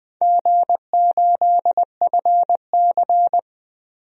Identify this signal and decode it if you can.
Morse code, heard as a keyed tone, about 20 wpm, 700 Hz. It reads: G8FC
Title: DXCC-100 Call Signs - 4 Characters - 20wpm